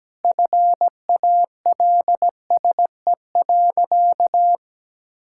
Morse code, heard as a keyed tone, about 17 wpm, 700 Hz. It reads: FALSE.
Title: Sentences from Top 1600 Words 17wpm